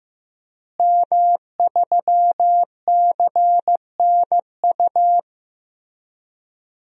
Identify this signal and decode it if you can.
Morse code, heard as a keyed tone, about 16 wpm, 700 Hz. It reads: M3CNU
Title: DXCC-100 Call Signs - 5 Characters - Repeated 3x - 15wpm